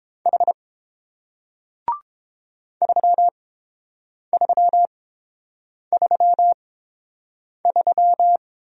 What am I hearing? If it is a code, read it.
Morse code, heard as a keyed tone, about 30 wpm, spaced as for 7 wpm, 700 Hz. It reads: H 3333